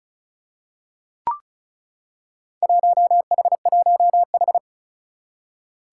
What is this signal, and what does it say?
Morse code, heard as a keyed tone, about 34 wpm, 700 Hz. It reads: 1H1H
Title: Four Challenging Letter-Number Combinations 35wpm